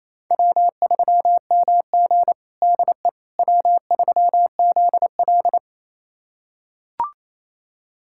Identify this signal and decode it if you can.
Morse code, heard as a keyed tone, about 28 wpm, 700 Hz. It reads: W3MG DE W3ZL
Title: USA Callsign DE Callsign - 4 Characters - 28wpm